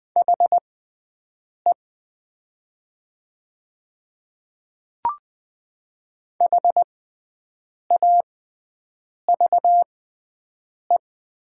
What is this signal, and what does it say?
Morse code, heard as a keyed tone, about 20 wpm, spaced as for 7 wpm, 700 Hz. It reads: HE HAVE
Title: 6x Character Spacing - Top 100 Words - 20wpm